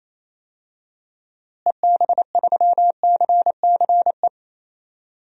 Morse code, heard as a keyed tone, about 28 wpm, 700 Hz. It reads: EB3CCE
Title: DXCC-100 Call Signs - 6 Characters - Repeated 3x - 28wpm